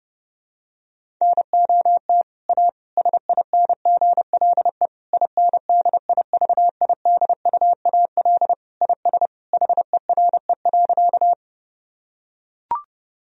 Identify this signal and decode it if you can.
Morse code, heard as a keyed tone, about 30 wpm, 700 Hz. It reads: NOT A SINGLE INDIVIDUAL IS HERE.
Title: Sentences from Top 1000 Words - Review Part 4 - 30wpm